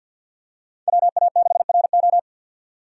Morse code, heard as a keyed tone, about 51 wpm, 700 Hz. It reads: WA6RO
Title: USA Call Signs - 5 Characters - 50wpm